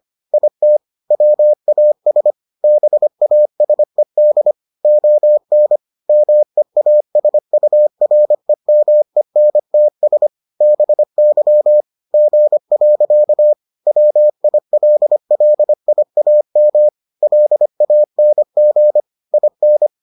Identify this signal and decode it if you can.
Morse code, heard as a keyed tone, about 25 wpm, 600 Hz. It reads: IT WAS BASED ON MEASUREMENTS BY G. WILLIAM LANG IN